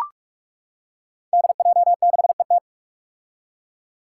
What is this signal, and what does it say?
Morse code, heard as a keyed tone, about 44 wpm, 700 Hz. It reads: DJ6ET
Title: Five Challenging Letter-Number Combinations 45wpm